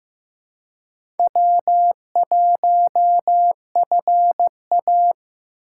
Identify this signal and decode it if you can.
Morse code, heard as a keyed tone, about 15 wpm, 700 Hz. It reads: W1FA
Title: USA Call Signs - 4 Characters - 2x Character Spacing - 15wpm